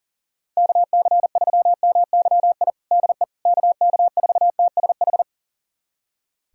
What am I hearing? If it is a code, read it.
Morse code, heard as a keyed tone, about 40 wpm, 700 Hz. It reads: KC3MYI DE KK4TSH
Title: USA Callsign DE Callsign - 6 Characters - 40wpm